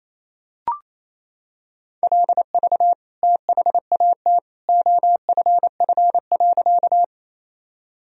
Morse code, heard as a keyed tone, about 27 wpm, 700 Hz. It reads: LV THAT OFF.
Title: Sentences from Top 400 Words and Abbreviations 28wpm